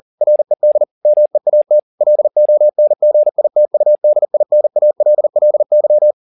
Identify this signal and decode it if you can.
Morse code, heard as a keyed tone, about 40 wpm, 600 Hz. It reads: RED MEAT LONGITUDINALLY